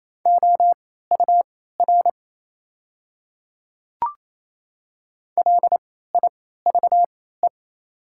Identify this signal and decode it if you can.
Morse code, heard as a keyed tone, about 28 wpm, spaced as for 16 wpm, 700 Hz. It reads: OUR LIVE
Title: 3x Character Spacing - Top 200 Words - 28wpm